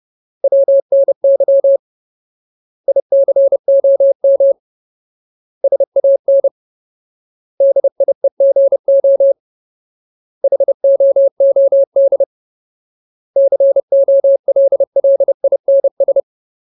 Morse code, heard as a keyed tone, about 30 wpm, 550 Hz. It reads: WNY ICOM SAN DIEGO HOOD COLLINS
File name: Ultimate Word List / 30WPM Ultimate